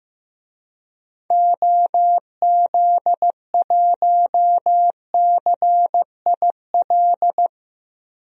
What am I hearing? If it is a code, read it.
Morse code, heard as a keyed tone, about 15 wpm, 700 Hz. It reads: OZ1CIL
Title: DXCC-100 Call Signs - 6 Characters - Repeated 2x - 15wpm